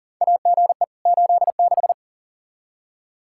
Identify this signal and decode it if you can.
Morse code, heard as a keyed tone, about 40 wpm, 700 Hz. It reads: AGE 86
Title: K1USN SST QSO Practice 40wpm